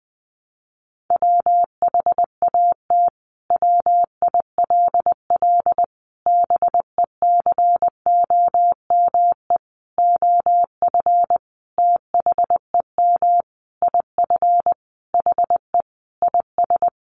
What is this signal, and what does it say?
Morse code, heard as a keyed tone, about 20 wpm, 700 Hz. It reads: WHAT WILL BECOME OF THEM IF HE IS
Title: Sentences from Top 1100 Words 20wpm